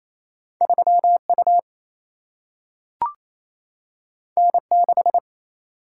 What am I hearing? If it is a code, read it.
Morse code, heard as a keyed tone, about 28 wpm, 700 Hz. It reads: 3 U N 6
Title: Single-Letter Single-Number 28wpm